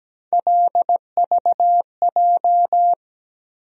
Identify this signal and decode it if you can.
Morse code, heard as a keyed tone, about 17 wpm, 700 Hz. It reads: LVJ
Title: USA Call Sign Postfixes - 3 Letters 17wpm